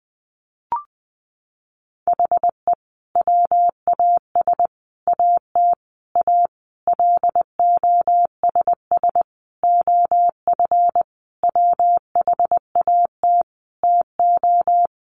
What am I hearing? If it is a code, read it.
Morse code, heard as a keyed tone, about 20 wpm, 700 Hz. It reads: HE WAS AT A LOSS OF WHAT TO
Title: Sentences from Top 1400 Words 20wpm